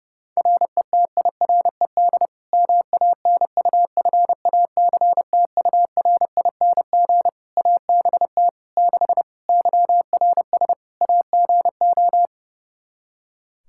Morse code, heard as a keyed tone, about 32 wpm, 700 Hz. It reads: RETIRED MANUFACTURING ABT 6 YRS AGO
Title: Practice QSO - Exchange 3 - More Info - 30wpm